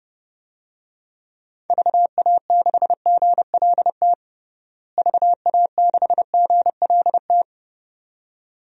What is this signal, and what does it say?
Morse code, heard as a keyed tone, about 30 wpm, 700 Hz. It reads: VA6GLT VA6GLT
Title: DXCC-100 Call Signs - 6 Characters - Repeated 2x - 30wpm